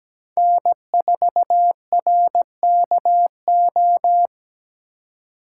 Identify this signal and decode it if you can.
Morse code, heard as a keyed tone, about 17 wpm, 700 Hz. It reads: N4RKO
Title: USA Call Signs - 5 Characters - 3x Character Spacing - 17wpm